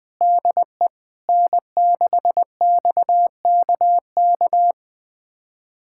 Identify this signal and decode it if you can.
Morse code, heard as a keyed tone, about 21 wpm, 700 Hz. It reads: DE N6XKK
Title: USA Callsign DE Callsign - 5 Characters - 20wpm